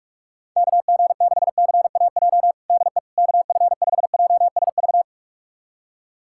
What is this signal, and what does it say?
Morse code, heard as a keyed tone, about 42 wpm, 700 Hz. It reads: KG6CAJ DE KF5JSV
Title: USA Callsign DE Callsign - 6 Characters - 45wpm